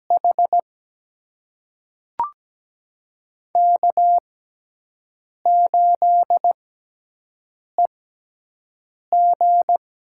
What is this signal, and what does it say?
Morse code, heard as a keyed tone, about 17 wpm, spaced as for 6 wpm, 700 Hz. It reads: H K8EG
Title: USA Call Signs - 4 Characters - 6x Character Spacing - 17wpm